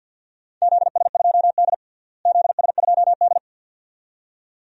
Morse code, heard as a keyed tone, about 49 wpm, 700 Hz. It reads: ZS2D ZS2D
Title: DXCC-100 Call Signs - 4 Characters - Repeated 2x - 50wpm